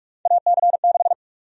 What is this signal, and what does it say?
Morse code, heard as a keyed tone, about 42 wpm, 700 Hz. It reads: AC6
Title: USA Call Sign Prefixes - 45wpm